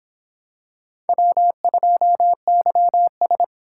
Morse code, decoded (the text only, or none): W2YS